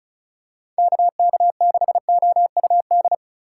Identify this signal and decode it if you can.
Morse code, heard as a keyed tone, about 34 wpm, 700 Hz. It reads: KK6OUD
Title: USA Call Signs - 6 Characters - 8x Character Spacing - 35wpm